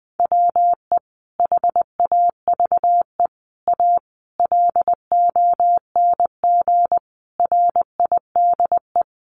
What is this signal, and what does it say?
Morse code, heard as a keyed tone, about 20 wpm, 700 Hz. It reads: WE HAVE A LONG RIDE
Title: Sentences from Top 1200 Words 20wpm